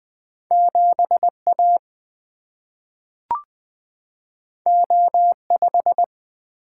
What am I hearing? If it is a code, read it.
Morse code, heard as a keyed tone, about 20 wpm, 700 Hz. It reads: 7 A O 5
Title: Single-Letter Single-Number 20wpm